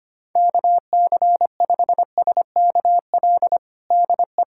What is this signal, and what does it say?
Morse code, heard as a keyed tone, about 25 wpm, 700 Hz. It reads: KC5SKL DE
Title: USA Callsign DE Callsign - 6 Characters - 25wpm